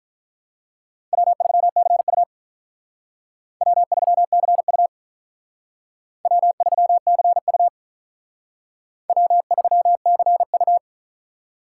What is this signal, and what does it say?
Morse code, heard as a keyed tone, about 47 wpm, 700 Hz. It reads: W3CU W3CU W3CU W3CU